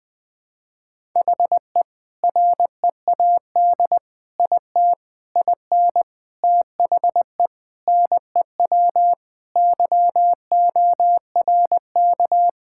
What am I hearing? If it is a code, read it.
Morse code, heard as a keyed tone, about 20 wpm, 700 Hz. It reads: HE READ IT IN THE NEW YORK
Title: Sentences with 4-Word Phrases - Spoken Beginning - 20wpm